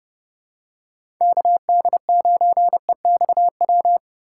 Morse code, heard as a keyed tone, about 30 wpm, 700 Hz. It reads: KD9EXW
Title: USA Call Signs - 6 Characters - 3x Character Spacing - 30wpm